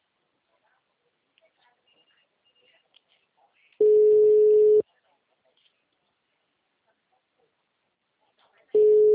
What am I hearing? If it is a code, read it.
Ringback tone (Europe, 425 Hz).